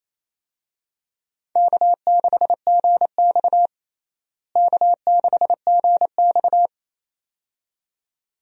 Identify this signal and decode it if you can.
Morse code, heard as a keyed tone, about 28 wpm, 700 Hz. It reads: K6GX K6GX
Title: USA Call Signs - 4 Characters - Repeated 2x - 28wpm